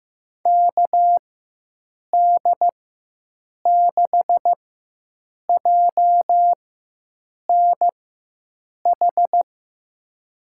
Morse code, heard as a keyed tone, about 15 wpm, spaced as for 7 wpm, 700 Hz. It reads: KD6JNH
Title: USA Call Signs - 6 Characters - 4x Character Spacing - 15wpm